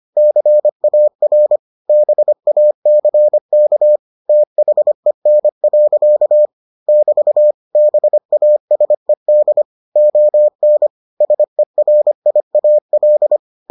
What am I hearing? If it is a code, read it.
Morse code, heard as a keyed tone, about 25 wpm, 600 Hz. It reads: CAR BACK THEN. <BT> BASED ON SERIAL